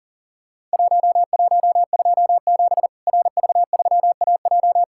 Morse code, heard as a keyed tone, about 40 wpm, 700 Hz. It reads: 1127 RV3AJ
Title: Worked All Europe DX Contest - QTC Exchange - 40wpm